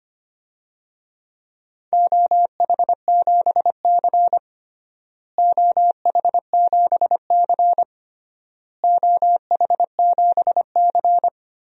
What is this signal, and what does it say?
Morse code, heard as a keyed tone, about 25 wpm, 700 Hz. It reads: OH7C OH7C OH7C
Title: DXCC-100 Call Signs - 4 Characters - Repeated 3x - 25wpm